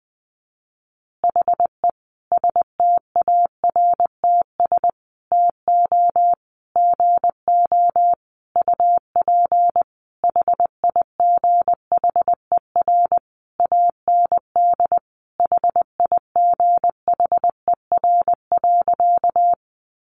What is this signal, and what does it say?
Morse code, heard as a keyed tone, about 19 wpm, 700 Hz. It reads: HE STARTS TO GO UP HIGHER AND HIGHER.
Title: Additional Easy Phrases 20wpm